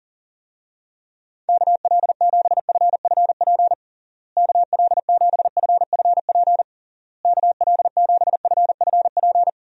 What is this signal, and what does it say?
Morse code, heard as a keyed tone, about 37 wpm, 700 Hz. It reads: KL7FFP KL7FFP KL7FFP
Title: USA Call Signs - 6 Characters - Repeated 3x - 40wpm